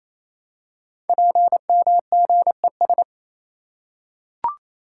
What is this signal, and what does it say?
Morse code, heard as a keyed tone, about 28 wpm, 700 Hz. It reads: PMGES
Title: Code Groups - 5 Letters - 28wpm